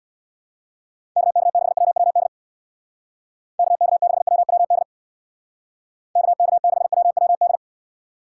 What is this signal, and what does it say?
Morse code, heard as a keyed tone, about 66 wpm, 700 Hz. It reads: KK6WWD KK6WWD KK6WWD